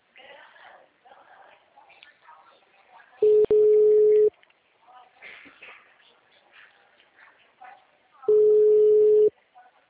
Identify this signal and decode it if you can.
Ringback tone (Europe, 425 Hz), with speech.
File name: out- -2003- - - .71612